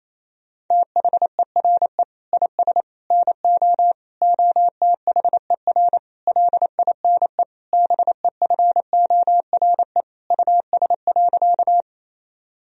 Morse code, heard as a keyed tone, about 27 wpm, 700 Hz. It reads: THERE IS NO OTHER LINE BEFORE US.
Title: Sentences from Top 1000 Words - Review Part 1 - 28wpm